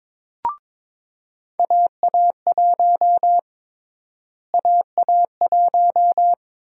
Morse code, heard as a keyed tone, about 22 wpm, 700 Hz. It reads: AA1 AA1
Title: USA Call Sign Prefixes - Repeated 2x - 22wpm